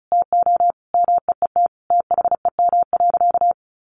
Morse code, heard as a keyed tone, about 34 wpm, 700 Hz. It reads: TO MEET THEM.
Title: Sentences from Top 1100 Words 35wpm